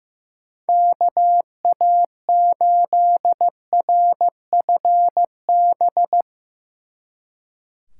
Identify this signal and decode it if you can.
Morse code, heard as a keyed tone, about 15 wpm, 700 Hz. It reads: KA8RFB
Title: Speed Racing - USA 2x3 Call Signs 15wpm